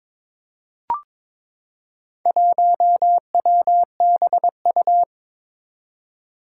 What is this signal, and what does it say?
Morse code, heard as a keyed tone, about 22 wpm, 700 Hz. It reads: 1WBU
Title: Four Challenging Letter-Number Combinations 22wpm